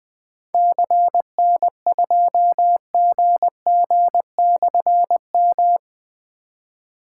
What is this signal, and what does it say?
Morse code, heard as a keyed tone, about 20 wpm, 700 Hz. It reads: CN2GG/M
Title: Difficult Call Signs 20wpm